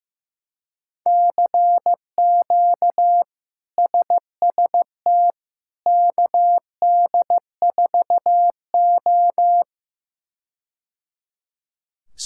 Morse code, heard as a keyed tone, about 15 wpm, 700 Hz. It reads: CQ SST KD4O
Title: K1USN SST QSO Practice - 15wpm